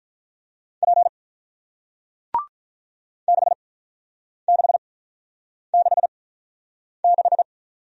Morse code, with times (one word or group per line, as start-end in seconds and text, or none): 0.8-1.1 P
3.3-7.4 6666